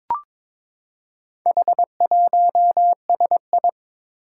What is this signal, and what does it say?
Morse code, heard as a keyed tone, about 22 wpm, 700 Hz. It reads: H1SI
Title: Four Challenging Letter-Number Combinations 22wpm